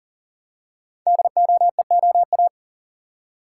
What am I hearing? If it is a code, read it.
Morse code, heard as a keyed tone, about 39 wpm, 690 Hz. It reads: DOEOA
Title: Code Groups - 5 Letters - 40wpm